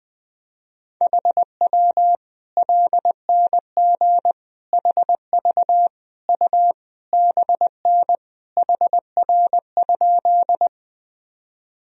Morse code, heard as a keyed tone, about 21 wpm, 700 Hz. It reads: HW LNG HV U BN HR?
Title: Sentences from Top 200 Words and Abbreviations 20wpm